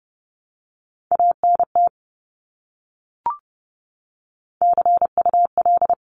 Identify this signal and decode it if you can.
Morse code, heard as a keyed tone, about 30 wpm, 700 Hz. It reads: ANT CUL
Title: Abbreviations Repeated 3x - 30wpm